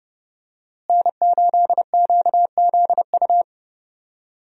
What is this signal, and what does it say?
Morse code, heard as a keyed tone, about 30 wpm, 700 Hz. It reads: N8QZU
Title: USA Call Signs - 5 Characters - Repeated 3x - 30wpm